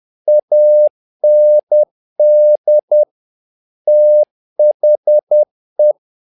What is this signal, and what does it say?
Morse code, heard as a keyed tone, about 10 wpm, 600 Hz. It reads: AND THE
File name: learning center / tower trouble / tower trouble 10wpm